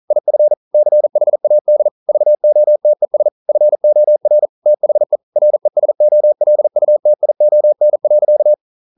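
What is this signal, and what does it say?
Morse code, heard as a keyed tone, about 42 wpm, 600 Hz. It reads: IF CHAD VOTES FOR THE RESOLUTION.
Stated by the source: Morse code practice at 41 wpm, from CW med Gnister.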